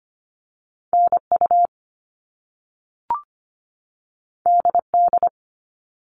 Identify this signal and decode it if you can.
Morse code, heard as a keyed tone, about 25 wpm, 700 Hz. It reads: NU DD
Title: All Cut Numbers - 2-Digit 25wpm